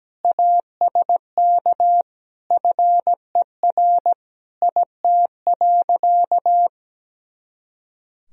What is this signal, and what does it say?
Morse code, heard as a keyed tone, about 17 wpm, 700 Hz. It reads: ASK FER IT.
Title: Sentences from Top 200 Words and Abbreviations 17wpm